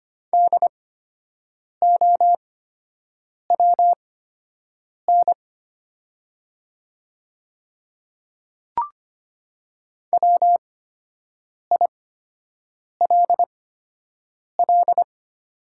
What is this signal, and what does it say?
Morse code, heard as a keyed tone, about 25 wpm, spaced as for 7 wpm, 700 Hz. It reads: DOWN WILL
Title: 8x Character Spacing - Top 100 Words - 25wpm